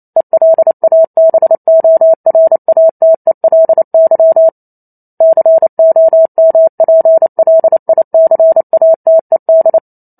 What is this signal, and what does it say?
Morse code, heard as a keyed tone, about 28 wpm, 650 Hz. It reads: ELABORATELY COMPLICATED
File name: Chapter 2 - 28 WPM